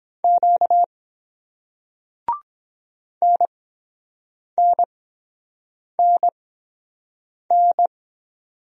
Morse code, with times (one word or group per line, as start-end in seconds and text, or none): 0.2-0.8 Q
3.2-7.9 NNNN